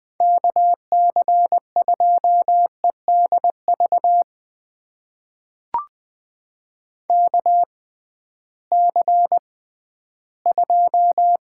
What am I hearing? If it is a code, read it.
Morse code, heard as a keyed tone, about 20 wpm, spaced as for 13 wpm, 700 Hz. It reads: KC2EDV KC2
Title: USA Call Signs - 6 Characters - 6x Character Spacing - 20wpm